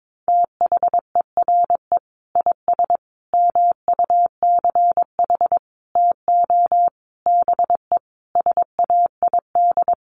Morse code, heard as a keyed tone, about 22 wpm, 700 Hz. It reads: THERE IS MUCH TO BE SAID
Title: Sentences from Top 1300 Words 22wpm